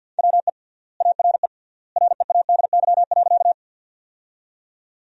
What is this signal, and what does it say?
Morse code, heard as a keyed tone, about 53 wpm, 700 Hz. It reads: WE ARE READY.
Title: 3x Word-Spacing - Sentences from Top 400 Words - 50wpm